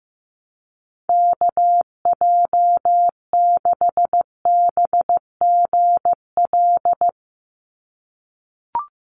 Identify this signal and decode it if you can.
Morse code, heard as a keyed tone, about 15 wpm, 700 Hz. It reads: KJ6BGL
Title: USA Call Signs 15wpm